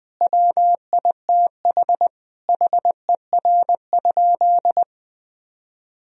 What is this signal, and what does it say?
Morse code, heard as a keyed tone, about 20 wpm, 700 Hz. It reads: WITH HER?
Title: Sentences from Top 1000 Words - Review Part 3 - 20wpm